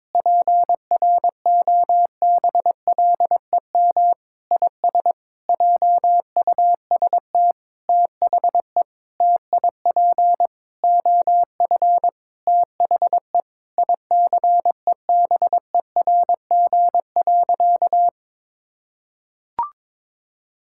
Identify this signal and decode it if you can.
Morse code, heard as a keyed tone, about 21 wpm, 700 Hz. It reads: PROBLEM IS JUST THE TIP OF THE ICEBERG.
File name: Sentences with 7-Word Phrases - 22wpm